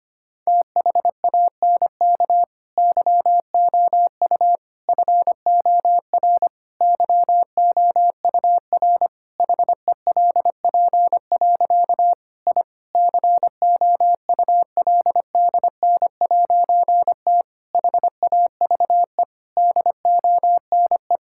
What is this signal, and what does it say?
Morse code, heard as a keyed tone, about 25 wpm, 700 Hz. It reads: THANK YOU FOR YOUR HELP. I COULDN'T HAVE DONE
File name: practice / 25wpm / Difficult Sentences with Hints - 25wpm